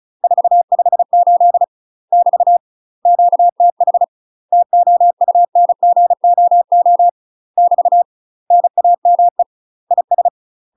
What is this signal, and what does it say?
Morse code, heard as a keyed tone, about 35 wpm, 700 Hz. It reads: 458 <BT> QTH TOUNGOO <BT> NAME IS
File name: UK MASTER QSOs TRAINING / 35WPM / UK MASTER QSOs 0007